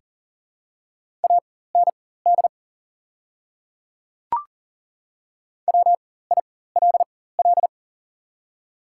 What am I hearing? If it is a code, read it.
Morse code, heard as a keyed tone, about 37 wpm, spaced as for 18 wpm, 700 Hz. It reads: AND WILL
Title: 4x Character Spacing - Top 100 Words - 40wpm